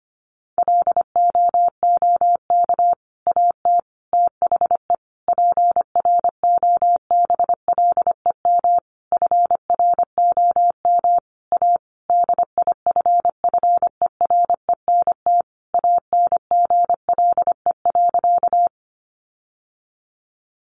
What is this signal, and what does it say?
Morse code, heard as a keyed tone, about 25 wpm, 700 Hz. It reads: LOOK AT THE PROBLEM FROM A DIFFERENT ANGLE.
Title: Sentences from Top 1500 Words 25wpm